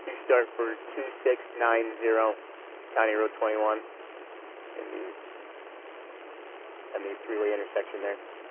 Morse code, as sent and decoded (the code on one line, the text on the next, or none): none